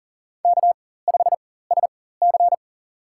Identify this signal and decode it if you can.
Morse code, heard as a keyed tone, about 40 wpm, spaced as for 19 wpm, 700 Hz. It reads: K5SC
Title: USA Call Signs - 4 Characters - 4x Character Spacing - 40wpm